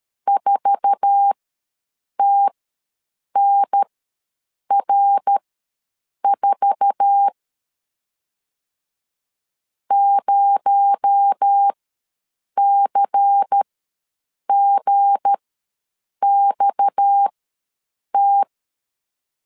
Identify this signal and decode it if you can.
Morse code, heard as a keyed tone, about 13 wpm, spaced as for 7 wpm, 800 Hz. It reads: 4TNR4 0CGXT
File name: LEZIONI / 8 YJ45 / 8 YJ4 ws 12wpm 3249